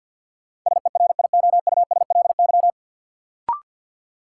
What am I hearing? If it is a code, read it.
Morse code, heard as a keyed tone, about 53 wpm, 700 Hz. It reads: SERIOUSLY
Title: Sentences from Top 1200 Words - New Word Drill Alt - 50wpm